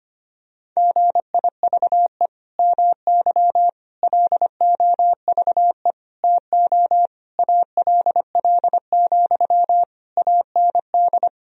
Morse code, heard as a keyed tone, about 25 wpm, 700 Hz. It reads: GIVE MY LOVE TO ALL, AND
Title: Sentences from Top 1000 Words - Review Part 4 - 25wpm